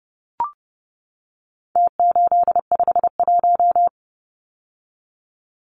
Morse code, heard as a keyed tone, about 30 wpm, 700 Hz. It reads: T851
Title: Numbers - Contest Style - 4-Digit 30wpm